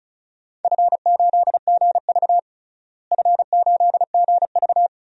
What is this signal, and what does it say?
Morse code, heard as a keyed tone, about 36 wpm, 700 Hz. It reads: F8GV F8GV
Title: DXCC-100 Call Signs - 4 Characters - Repeated 2x - 35wpm